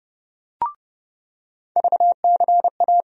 Morse code, heard as a keyed tone, about 30 wpm, 700 Hz. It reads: VCA
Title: USA Call Sign Postfixes - 3 Letters 30wpm